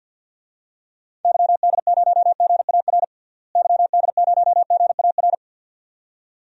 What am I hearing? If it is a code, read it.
Morse code, heard as a keyed tone, about 46 wpm, 700 Hz. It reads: YD0GAR YD0GAR
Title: DXCC-100 Call Signs - 6 Characters - Repeated 2x - 50wpm